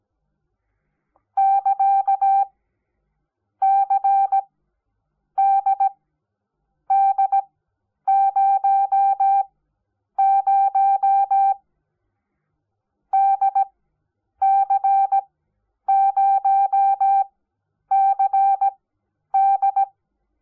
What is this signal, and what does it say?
Morse code, heard as a keyed tone, about 17 wpm, spaced as for 8 wpm, 780 Hz. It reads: <KA> CDD00 DC0CD
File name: Onlinelehrgang / morsen / DJ4UF Morsen05